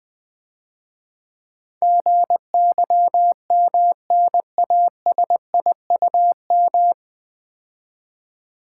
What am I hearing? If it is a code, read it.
Morse code, heard as a keyed tone, about 20 wpm, 700 Hz. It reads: GYMNASIUM
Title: Difficult Words - Speed-Racing with Hints - 20wpm